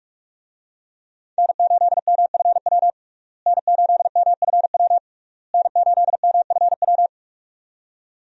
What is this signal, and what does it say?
Morse code, heard as a keyed tone, about 46 wpm, 700 Hz. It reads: N8MFW N8MFW N8MFW
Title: USA Call Signs - 5 Characters - Repeated 3x - 45wpm